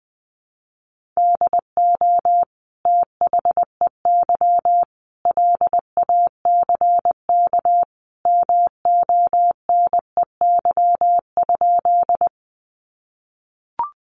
Morse code, heard as a keyed tone, about 20 wpm, 700 Hz. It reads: DO THEY LACK MONEY?
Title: Sentences from Top 1100 Words 20wpm